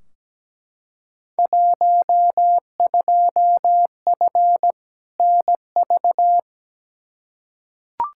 Morse code, heard as a keyed tone, about 17 wpm, 700 Hz. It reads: 12F NV
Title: Field Day Exchanges Encore - 17wpm